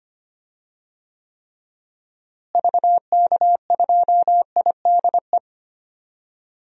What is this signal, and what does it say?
Morse code, heard as a keyed tone, about 25 wpm, 700 Hz. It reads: VK2IDE
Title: DXCC-100 Call Signs - 6 Characters - Repeated 2x - 25wpm